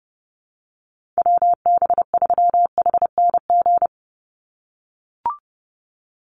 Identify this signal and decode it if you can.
Morse code, heard as a keyed tone, about 30 wpm, 700 Hz. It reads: WB3HNG
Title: USA Call Signs 30wpm